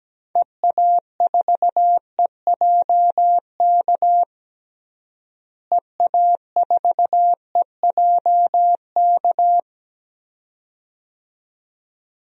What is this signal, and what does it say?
Morse code, heard as a keyed tone, about 17 wpm, 700 Hz. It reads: EA4EJK EA4EJK
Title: DXCC-100 Call Signs - 6 Characters - Repeated 2x - 17wpm